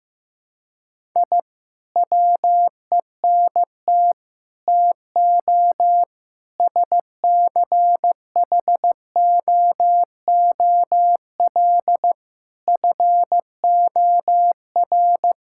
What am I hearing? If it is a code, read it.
Morse code, heard as a keyed tone, about 16 wpm, 700 Hz. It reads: I WENT TO SCHOOL FOR
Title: Sentences from Top 1000 Words - Review Part 6 - 15wpm